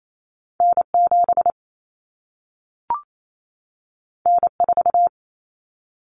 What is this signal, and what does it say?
Morse code, heard as a keyed tone, about 28 wpm, 700 Hz. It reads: N7 N4
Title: Common Cut Numbers - 2-Digit 28wpm